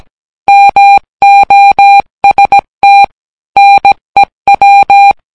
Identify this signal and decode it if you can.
Morse code, heard as a keyed tone, about 17 wpm, 800 Hz. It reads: MOST NEW